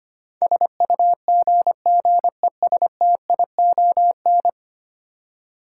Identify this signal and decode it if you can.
Morse code, heard as a keyed tone, about 25 wpm, 700 Hz. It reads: SUGGESTION
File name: 10 Letter Words - First 3 Letters Given - 25wpm